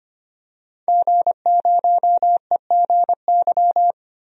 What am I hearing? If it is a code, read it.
Morse code, heard as a keyed tone, about 25 wpm, 700 Hz. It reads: G0EGY
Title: DXCC-100 Call Signs - 5 Characters - Repeated 3x - 25wpm